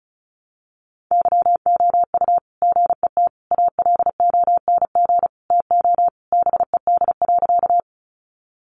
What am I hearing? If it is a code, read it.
Morse code, heard as a keyed tone, about 34 wpm, 700 Hz. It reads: YOU GET ALONG TO BED.
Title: Sentences from Top 700 Words 35wpm